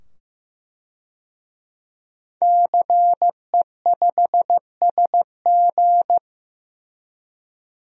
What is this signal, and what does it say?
Morse code, heard as a keyed tone, about 15 wpm, 700 Hz. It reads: CE5SG
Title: DXCC-100 Call Signs - 5 Characters - 15wpm